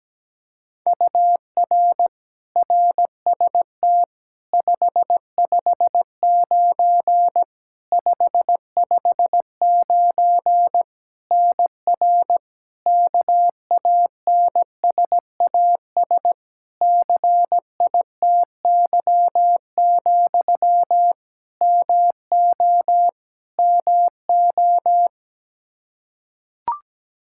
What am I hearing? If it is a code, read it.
Morse code, heard as a keyed tone, about 17 wpm, 700 Hz. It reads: UR RST 559 559 NR KANSAS CITY, MO MO
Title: K1USN SST QSO Practice 17wpm